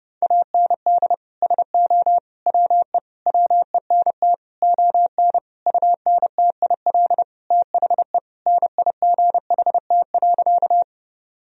Morse code, heard as a keyed tone, about 30 wpm, 700 Hz. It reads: AND SO WE WENT ON UNTIL THE NIGHT.
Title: Sentences from Top 1000 Words - Review Part 1 - 30wpm